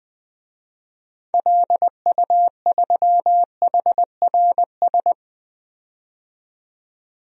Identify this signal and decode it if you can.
Morse code, heard as a keyed tone, about 20 wpm, 700 Hz. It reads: LU3HRS
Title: DXCC-100 Call Signs - 6 Characters - 20wpm